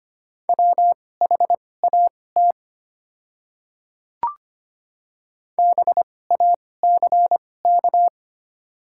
Morse code, heard as a keyed tone, about 25 wpm, spaced as for 18 wpm, 700 Hz. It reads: WHAT BACK
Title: 2x Character Spacing - Top 200 Words - 25wpm